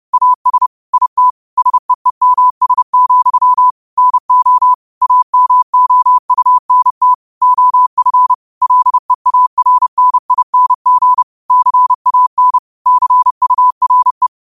Morse code, heard as a keyed tone, about 30 wpm, 1000 Hz. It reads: AS IT SEEMS, NO AMOUNT OF LEARNING CAN CURE
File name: qotdfiles / 30wpm